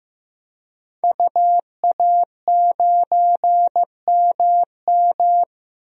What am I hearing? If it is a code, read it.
Morse code, heard as a keyed tone, about 15 wpm, 700 Hz. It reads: UA9MM
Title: DXCC-100 Call Signs - 5 Characters - 15wpm